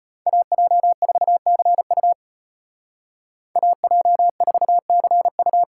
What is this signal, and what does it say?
Morse code, heard as a keyed tone, about 33 wpm, 700 Hz. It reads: AJ4CU AJ4CU